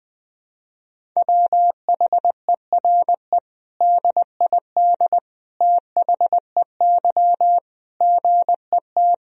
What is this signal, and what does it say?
Morse code, heard as a keyed tone, about 20 wpm, 700 Hz. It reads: WHERE DID THEY GET
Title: Sentences from Top 400 Words and Abbreviations 20wpm